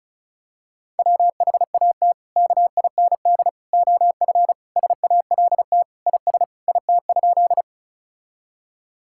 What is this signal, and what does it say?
Morse code, heard as a keyed tone, about 36 wpm, 700 Hz. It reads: WHAT KIND OF SALT IS IT?
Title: Sentences from Top 1000 Words - Review Part 5 - 35wpm